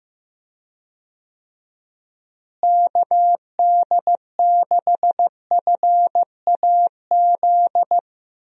Morse code, heard as a keyed tone, about 15 wpm, 700 Hz. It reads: KD6FAZ
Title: USA Call Signs - 6 Characters - Repeated 2x - 15wpm